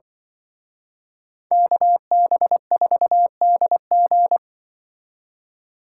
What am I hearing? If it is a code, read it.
Morse code, heard as a keyed tone, about 24 wpm, 700 Hz. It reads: KB4DG
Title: Speed Racing - USA 2x2 Call Signs 20wpm